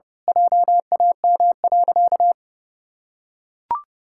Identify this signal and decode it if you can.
Morse code, heard as a keyed tone, about 30 wpm, 700 Hz. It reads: JAM.
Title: Pangram Challenge v2 - 30wpm